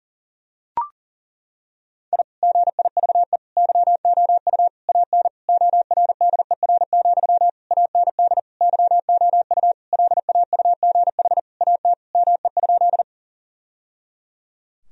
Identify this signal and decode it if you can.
Morse code, heard as a keyed tone, about 37 wpm, 700 Hz. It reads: I GIVE YOU AN ORDER, AND YOU LAUGH AT ME?
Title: Sentences from Top 1000 Words - Review Part 2 - 40wpm